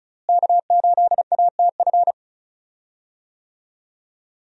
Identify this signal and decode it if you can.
Morse code, heard as a keyed tone, about 36 wpm, 700 Hz. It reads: K8ATF
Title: USA Call Signs - 5 Characters - Repeated 2x - 35wpm